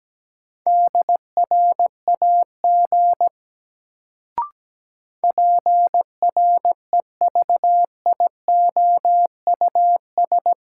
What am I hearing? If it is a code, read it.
Morse code, heard as a keyed tone, about 17 wpm, 700 Hz. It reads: DRAG PREVIOUS
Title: Sentences from Top 1500 Words - New Word Drill Alt - 17wpm